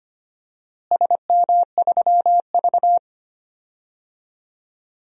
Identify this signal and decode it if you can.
Morse code, heard as a keyed tone, about 25 wpm, 700 Hz. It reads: SM3V
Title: DXCC-100 Call Signs - 4 Characters - Repeated 3x - 25wpm